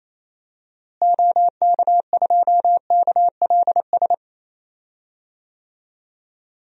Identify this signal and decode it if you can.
Morse code, heard as a keyed tone, about 28 wpm, 700 Hz. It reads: OK2KLS
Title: DXCC-100 Call Signs - 6 Characters - Repeated 3x - 28wpm